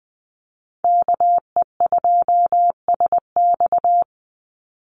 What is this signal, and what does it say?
Morse code, heard as a keyed tone, about 20 wpm, 700 Hz. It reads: KE2SX
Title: Call Signs - 20wpm